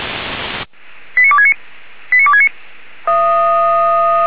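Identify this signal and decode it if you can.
Five-tone selective call (ZVEI), heard as quick runs of five tones, tones of about 70 ms. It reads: 89269 89269
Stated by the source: ZVEI five-tone call 89287 received by SDR, from the Feuerwehr Orlamünde audio archive.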